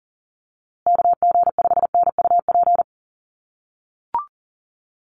Morse code, heard as a keyed tone, about 40 wpm, 700 Hz. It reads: KG5NUP
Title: USA Call Signs Repeated 2x 40wpm